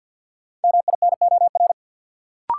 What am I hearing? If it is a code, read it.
Morse code, heard as a keyed tone, about 52 wpm, 700 Hz. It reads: MINOR
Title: Sentences from Top 1500 Words - New Word Drill Alt - 50wpm